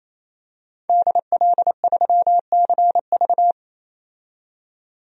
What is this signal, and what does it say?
Morse code, heard as a keyed tone, about 27 wpm, 690 Hz. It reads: DL3CV
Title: DXCC-100 Call Signs - 5 Characters - 28wpm